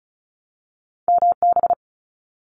Morse code, heard as a keyed tone, about 36 wpm, 700 Hz. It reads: MB
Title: Canadian Provinces and Territories 35wpm